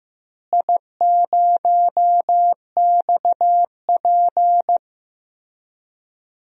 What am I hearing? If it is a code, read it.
Morse code, heard as a keyed tone, about 15 wpm, 700 Hz. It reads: I0XP
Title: DXCC-100 Call Signs - 4 Characters - Repeated 3x - 15wpm